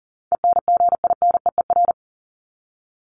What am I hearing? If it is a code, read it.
Morse code, heard as a keyed tone, about 40 wpm, 700 Hz. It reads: ENGINEER
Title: ING Suffix - Top 100 Words with Base Words 40wpm